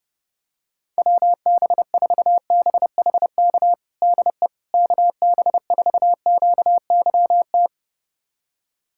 Morse code, heard as a keyed tone, about 30 wpm, 700 Hz. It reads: WB4BHK DE KB4QYT
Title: USA Callsign DE Callsign - 6 Characters - 30wpm